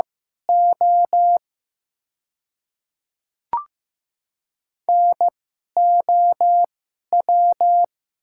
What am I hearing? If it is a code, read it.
Morse code, heard as a keyed tone, about 15 wpm, spaced as for 11 wpm, 700 Hz. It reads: O NOW
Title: 2x Character Spacing - Top 100 Words - 15wpm